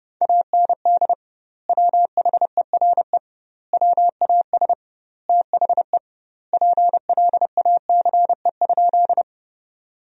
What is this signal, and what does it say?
Morse code, heard as a keyed tone, about 30 wpm, 700 Hz. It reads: AND WHERE WAS THE PLACE?
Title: 2x Word-Spacing - Sentences from Top 200 Words - 30wpm